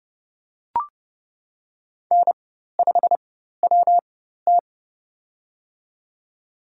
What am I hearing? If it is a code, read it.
Morse code, heard as a keyed tone, about 30 wpm, spaced as for 14 wpm, 700 Hz. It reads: N5WT
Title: USA Call Signs - 4 Characters - 4x Character Spacing - 30wpm